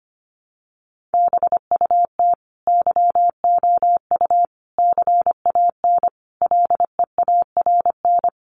Morse code, heard as a keyed tone, about 25 wpm, 700 Hz. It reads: BUT YOU CAN LEARN
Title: Sentences from Top 1500 Words 25wpm